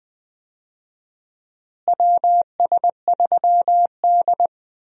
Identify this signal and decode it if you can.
Morse code, heard as a keyed tone, about 20 wpm, 700 Hz. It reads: WS3D
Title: USA Call Signs - 4 Characters - Repeated 2x - 20wpm